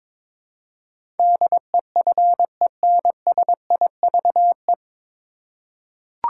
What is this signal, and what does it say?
Morse code, heard as a keyed tone, about 22 wpm, 700 Hz. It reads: DEFENSIVE
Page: Morse Code Ninja practice 22wpm